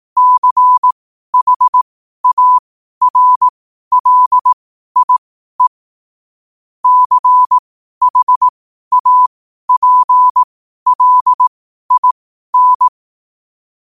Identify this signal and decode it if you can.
Morse code, heard as a keyed tone, about 18 wpm, spaced as for 12 wpm, 1000 Hz. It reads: CHARLIE CHAPLIN
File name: qotdfiles / 12wpm